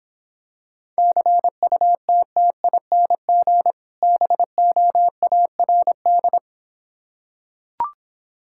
Morse code, heard as a keyed tone, about 26 wpm, 700 Hz. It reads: CUTTING BOARD